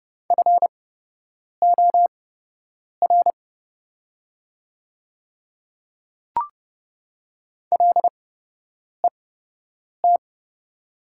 Morse code, heard as a keyed tone, about 30 wpm, spaced as for 8 wpm, 700 Hz. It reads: FOR LET
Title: 8x Character Spacing - Top 500 Words - 30wpm